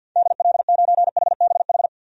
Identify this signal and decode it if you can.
Morse code, heard as a keyed tone, about 52 wpm, 700 Hz. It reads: DL9HBH
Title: DXCC-100 Call Signs - 6 Characters - 50wpm